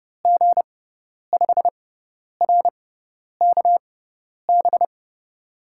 Morse code, heard as a keyed tone, about 30 wpm, spaced as for 10 wpm, 700 Hz. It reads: G5RKB
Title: USA Call Signs - 6 Characters - 6x Character Spacing - 30wpm